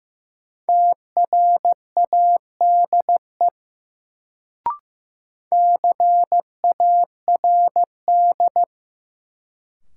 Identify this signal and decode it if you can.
Morse code, heard as a keyed tone, about 15 wpm, 700 Hz. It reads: TRADE CARD
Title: Sentences from Top 600 Words - New Word Drill - 15wpm